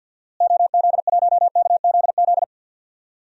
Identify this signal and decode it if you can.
Morse code, heard as a keyed tone, about 52 wpm, 700 Hz. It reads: OZ1KZZ
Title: DXCC-100 Call Signs - 6 Characters - Repeated 2x - 50wpm